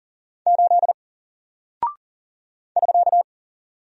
Morse code, heard as a keyed tone, about 40 wpm, 700 Hz. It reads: 8 <SK>
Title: Single Character (Letter:Number:Punctuation:Prosign) 40wpm